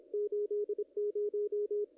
Morse code, heard as a keyed tone, about 26 wpm, 440 Hz, with radio noise. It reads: 80
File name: contests / ss-cw-k5tr-op-n5rz -0100